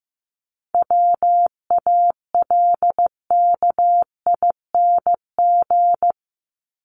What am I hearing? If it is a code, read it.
Morse code, heard as a keyed tone, about 15 wpm, 700 Hz. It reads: WALKING
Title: ING Suffix - Top 100 Words with Base Words 15wpm